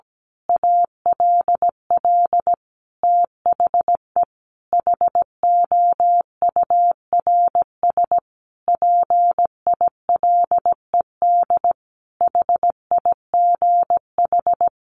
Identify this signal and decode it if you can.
Morse code, heard as a keyed tone, about 17 wpm, 700 Hz. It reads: ALL THE HOURS PILED HIGH
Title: A Sound Of Thunder 17wpm